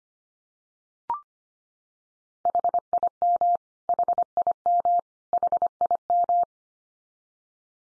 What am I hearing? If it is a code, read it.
Morse code, heard as a keyed tone, about 25 wpm, 700 Hz. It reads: HIM HIM HIM
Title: Top 100 Words 25wpm x3